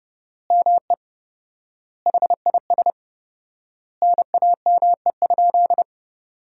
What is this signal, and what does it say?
Morse code, heard as a keyed tone, about 30 wpm, 700 Hz. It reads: ME HIS NAME?
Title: 4x Word-Spacing - Sentences from Top 400 Words - 30wpm